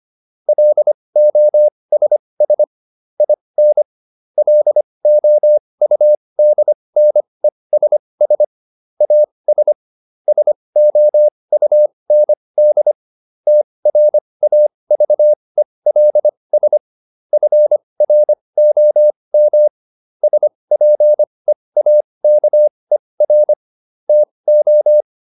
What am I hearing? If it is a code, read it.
Morse code, heard as a keyed tone, about 25 wpm, spaced as for 20 wpm, 600 Hz. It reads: LOSS IN LOUDNESS AS SOUND TRAVELS FROM SPEAKER TO